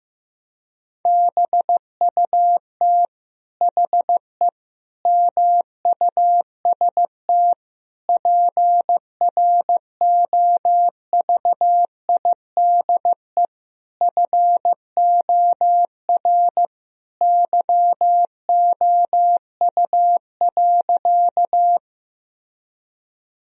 Morse code, heard as a keyed tone, about 15 wpm, 700 Hz. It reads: BUT HE MUST PROVIDE FOR YOU.
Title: Sentences from Top 1000 Words - Review Part 3 - 15wpm